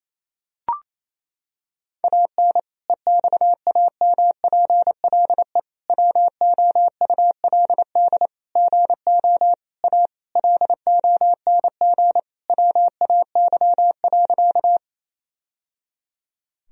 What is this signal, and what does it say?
Morse code, heard as a keyed tone, about 28 wpm, 700 Hz. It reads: AN EXAMPLE WOULD GO A LONG WAY.
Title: Sentences from Top 1000 Words - Review Part 1 - 28wpm